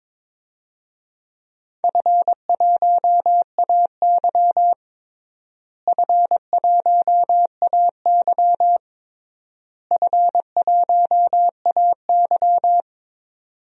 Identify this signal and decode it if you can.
Morse code, heard as a keyed tone, about 21 wpm, 700 Hz. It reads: F1AY F1AY F1AY
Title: DXCC-100 Call Signs - 4 Characters - Repeated 3x - 22wpm